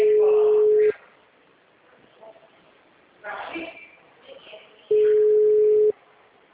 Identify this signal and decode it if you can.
Ringback tone (Europe, 425 Hz), with speech.
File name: out- -2032- - - .6667